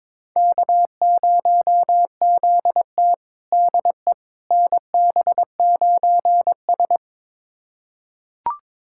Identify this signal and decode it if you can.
Morse code, heard as a keyed tone, about 22 wpm, 700 Hz. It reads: K0ZT DE NB9S
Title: USA Callsign DE Callsign - 4 Characters - 22wpm